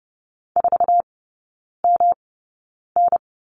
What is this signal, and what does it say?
Morse code, heard as a keyed tone, about 30 wpm, spaced as for 9 wpm, 700 Hz. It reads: 4MN
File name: Lesson 18e - Letter m - Three Characters 30wpm